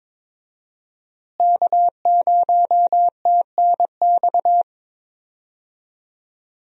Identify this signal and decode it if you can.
Morse code, heard as a keyed tone, about 22 wpm, 700 Hz. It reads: K0TNX
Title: USA Call Signs - 5 Characters - Repeated 3x - 22wpm